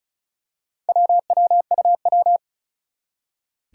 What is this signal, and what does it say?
Morse code, heard as a keyed tone, about 36 wpm, 700 Hz. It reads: WWUW
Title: Four Challenging Letter-Number Combinations 35wpm